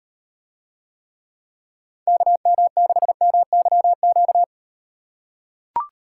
Morse code, heard as a keyed tone, about 38 wpm, 700 Hz. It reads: KM6MYQ